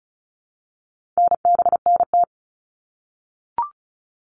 Morse code, heard as a keyed tone, about 34 wpm, 700 Hz. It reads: NBNT
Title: All Cut Numbers - 4-Digit 35wpm